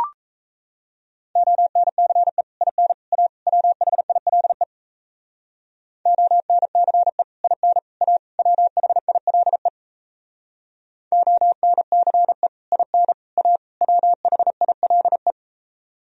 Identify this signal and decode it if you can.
Morse code, heard as a keyed tone, about 35 wpm, 690 Hz. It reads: ONCE IN A WHILE ONCE IN A WHILE ONCE IN A WHILE
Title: Speed-Racing - Sets of 4 Words 28wpm